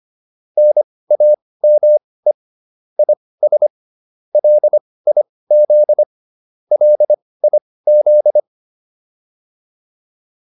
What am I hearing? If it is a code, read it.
Morse code, heard as a keyed tone, about 25 wpm, spaced as for 18 wpm, 600 Hz. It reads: NAME IS LIZ LIZ
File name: QSO304 18